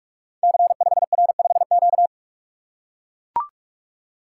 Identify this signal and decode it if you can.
Morse code, heard as a keyed tone, about 44 wpm, 700 Hz. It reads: C5R5Q
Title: Code Groups - 5 Letter:Numbers - 45wpm